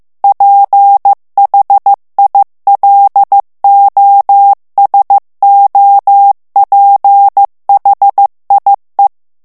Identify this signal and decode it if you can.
Morse code, heard as a keyed tone, about 15 wpm, 800 Hz. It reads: PHILOSOPHIE
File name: FRANCAIS TEXTE CLAIR 2 VITESSE 14 MOTS MINUTES